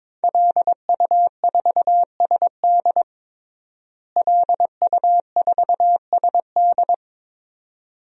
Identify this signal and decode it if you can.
Morse code, heard as a keyed tone, about 22 wpm, 700 Hz. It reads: LU4SD LU4SD
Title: DXCC-100 Call Signs - 5 Characters - Repeated 2x - 22wpm